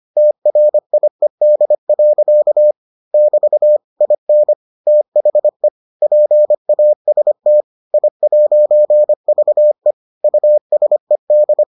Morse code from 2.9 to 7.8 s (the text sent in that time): <BT> IN THE PAST